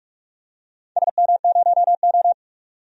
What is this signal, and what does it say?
Morse code, heard as a keyed tone, about 46 wpm, 700 Hz. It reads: SM0O
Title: DXCC-100 Call Signs - 4 Characters - 45wpm